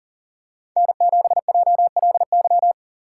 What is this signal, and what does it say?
Morse code, heard as a keyed tone, about 38 wpm, 700 Hz. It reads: N7JLY
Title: USA Call Signs - 2x Character Spacing - 40wpm